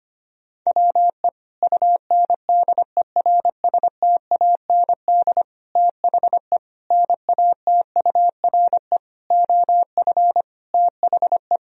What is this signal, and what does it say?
Morse code, heard as a keyed tone, about 25 wpm, 700 Hz. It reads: WE UNDERSTAND THE NATURE OF THE
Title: Sentences from Top 1000 Words - Review Part 2 - 25wpm